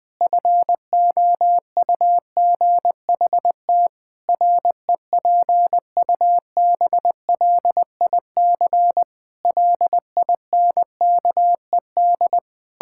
Morse code, heard as a keyed tone, about 21 wpm, 700 Hz. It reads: FOUGHT REPUBLIC LINKED
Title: Pangram Challenge v2 - 20wpm